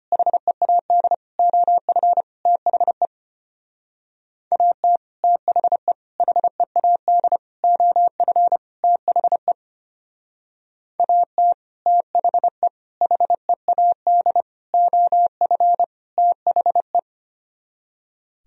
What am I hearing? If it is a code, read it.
Morse code, heard as a keyed tone, about 33 wpm, 700 Hz. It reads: HEAD OF THE AT THE HEAD OF THE AT THE HEAD OF THE